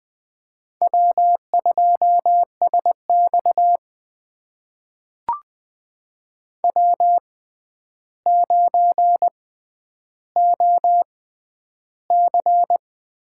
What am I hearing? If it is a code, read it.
Morse code, heard as a keyed tone, about 20 wpm, spaced as for 10 wpm, 700 Hz. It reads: W2SX W9OC
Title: USA Call Signs - 4 Characters - 6x Character Spacing - 20wpm